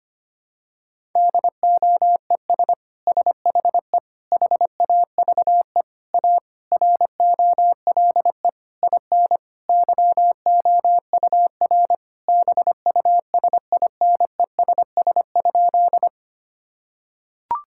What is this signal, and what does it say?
Morse code, heard as a keyed tone, about 25 wpm, 700 Hz. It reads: DOES SHE HAVE A ROLE IN YOUR BUSINESS?
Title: Sentences from Top 1000 Words - Review Part 6 - 25wpm